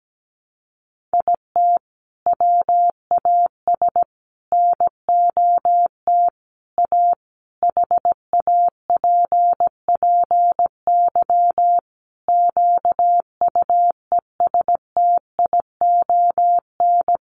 Morse code, heard as a keyed tone, about 17 wpm, 700 Hz. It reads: IT WAS NOT A HAPPY QUESTION
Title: Sentences from Top 1000 Words 17wpm